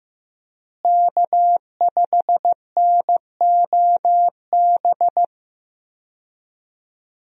Morse code, heard as a keyed tone, about 15 wpm, 700 Hz. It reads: K5NOB
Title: USA Call Signs - 5 Characters - Repeated 3x - 15wpm